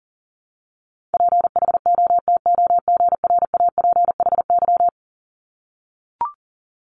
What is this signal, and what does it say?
Morse code, heard as a keyed tone, about 42 wpm, 690 Hz. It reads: PHOTOGRAPHY
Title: Multi-letter Phonemes 40wpm